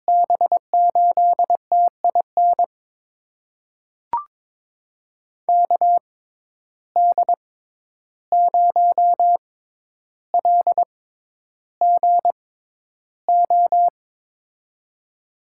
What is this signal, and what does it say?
Morse code, heard as a keyed tone, about 21 wpm, spaced as for 11 wpm, 700 Hz. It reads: B8TIN KD0LGO
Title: USA Call Signs - 6 Characters - 6x Character Spacing - 22wpm